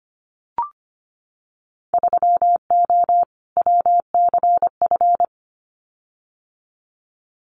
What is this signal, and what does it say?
Morse code, heard as a keyed tone, about 25 wpm, 700 Hz. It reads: 3O WCF
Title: Winter Field Day Exchanges 25wpm